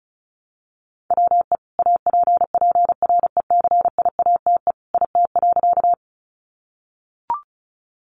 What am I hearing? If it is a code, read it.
Morse code, heard as a keyed tone, about 34 wpm, 700 Hz. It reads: WE APPRECIATE IT.
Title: Sentences from Top 1300 Words 35wpm